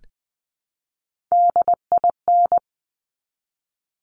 Morse code, heard as a keyed tone, about 20 wpm, 700 Hz. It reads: DIN
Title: Top Three Letter Combinations in Words 20wpm